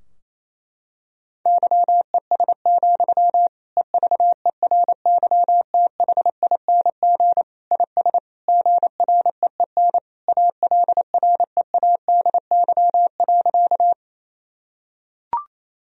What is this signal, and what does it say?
Morse code, heard as a keyed tone, about 28 wpm, 700 Hz. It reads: YES, EVERYTHING IS GREEN ALREADY.
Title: Sentences from Top 1000 Words - Review Part 2 - 28wpm